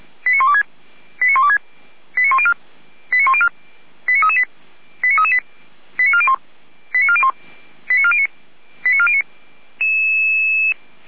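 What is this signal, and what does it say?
Five-tone selective call (ZVEI), heard as quick runs of five tones, tones of about 70 ms. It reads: 89126 89126 89104 89104 89338 89338 89401 89401 89409 89409
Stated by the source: ZVEI five-tone selective call received by SDR, from the Feuerwehr Orlamünde fire brigade server.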